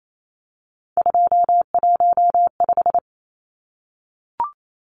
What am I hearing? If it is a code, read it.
Morse code, heard as a keyed tone, about 28 wpm, 700 Hz. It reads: 215
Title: Numbers - 3-Digit 28wpm